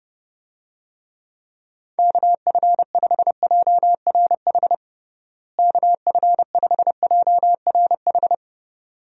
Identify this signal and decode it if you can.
Morse code, heard as a keyed tone, about 30 wpm, 700 Hz. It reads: KF5JRH KF5JRH
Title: USA Call Signs - 6 Characters - Repeated 2x - 30wpm